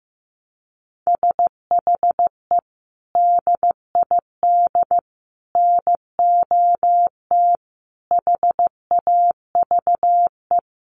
Morse code, heard as a keyed tone, about 15 wpm, 700 Hz. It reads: SHE DID NOT HAVE
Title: Sentences from Top 1300 Words 15wpm